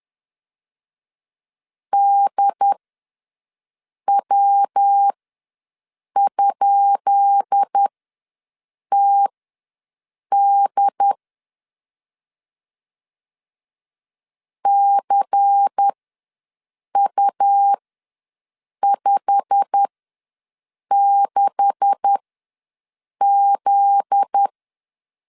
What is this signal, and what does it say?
Morse code, heard as a keyed tone, about 10 wpm, spaced as for 6 wpm, 800 Hz. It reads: DW?TD CU56Z
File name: LEZIONI / 10 SIGNS / 10C DOT ws 10wpm 4213